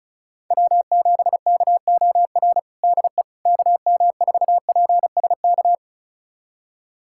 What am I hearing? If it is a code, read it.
Morse code, heard as a keyed tone, about 36 wpm, 700 Hz. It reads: W7KOR DE KM4PSK
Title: USA Callsign DE Callsign 35wpm